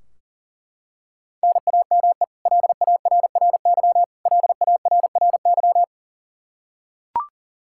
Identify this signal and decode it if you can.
Morse code, heard as a keyed tone, about 40 wpm, 700 Hz. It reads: NAME LARRY LARRY
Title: Practice QSO - Exchange 1 - Beginning - 40wpm